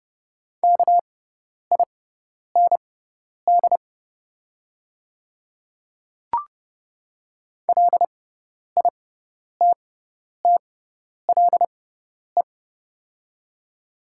Morse code, heard as a keyed tone, about 32 wpm, spaced as for 10 wpm, 700 Hz. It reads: KIND LITTLE
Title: 6x Character Spacing - Top 300 Words - 30wpm